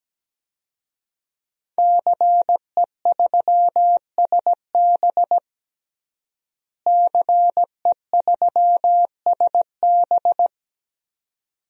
Morse code, heard as a keyed tone, about 17 wpm, 700 Hz. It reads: CE3SB CE3SB
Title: DXCC-100 Call Signs - 5 Characters - Repeated 2x - 17wpm